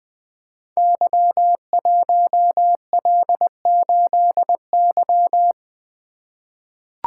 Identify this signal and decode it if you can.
Morse code, heard as a keyed tone, about 20 wpm, 700 Hz. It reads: Y1L8Y
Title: Code Groups - 5 Letter:Numbers - 20wpm